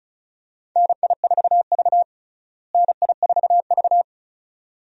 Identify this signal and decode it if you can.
Morse code, heard as a keyed tone, about 35 wpm, 700 Hz. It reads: NI4V NI4V
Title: USA Call Signs - 4 Characters - Repeated 2x - 35wpm